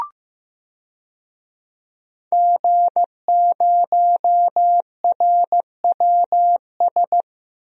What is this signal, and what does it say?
Morse code, heard as a keyed tone, about 15 wpm, 700 Hz. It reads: G0RWS
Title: DXCC-100 Call Signs - 5 Characters - Repeated 2x - 15wpm